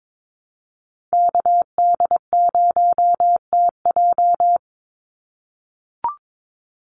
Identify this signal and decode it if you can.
Morse code, heard as a keyed tone, about 22 wpm, 700 Hz. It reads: KD0TJ
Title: USA Call Signs 22wpm